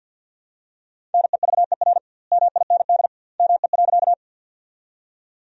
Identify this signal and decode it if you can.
Morse code, heard as a keyed tone, about 51 wpm, 700 Hz. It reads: NEVER MIND ME.
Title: 2x Word-Spacing - Sentences from Top 500 Words - 50wpm